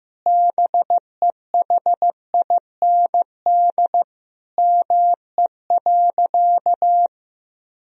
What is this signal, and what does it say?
Morse code, heard as a keyed tone, about 15 wpm, 700 Hz. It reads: BEHIND ME.
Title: Sentences from Top 1000 Words - Review Part 1 - 15wpm